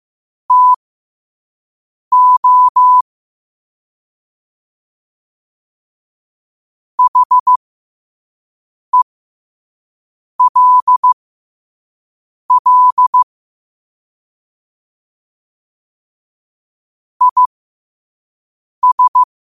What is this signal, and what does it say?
Morse code, heard as a keyed tone, about 15 wpm, spaced as for 5 wpm, 1000 Hz. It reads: TO HELL IS
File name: qotdfiles / 5wpm